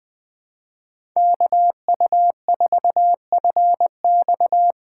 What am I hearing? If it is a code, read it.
Morse code, heard as a keyed tone, about 20 wpm, 700 Hz. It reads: KU4FX
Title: USA Call Signs - 5 Characters - 3x Character Spacing - 20wpm